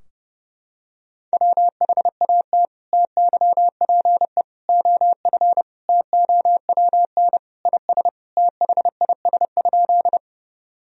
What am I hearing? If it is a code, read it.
Morse code, heard as a keyed tone, about 32 wpm, 700 Hz. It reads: WHAT TYPE OF TOWN IS THIS?
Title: Sentences from Top 1000 Words - Review Part 3 - 30wpm